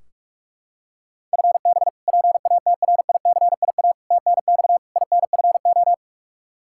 Morse code, heard as a keyed tone, about 46 wpm, 700 Hz. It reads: FB PATRICIA TNX INFO
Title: Practice QSO - Exchange 3 - More Info - 45wpm